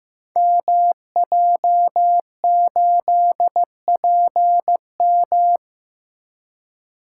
Morse code, heard as a keyed tone, about 15 wpm, 700 Hz. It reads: MJ8PM
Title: Code Groups - 5 Letter:Numbers - 15wpm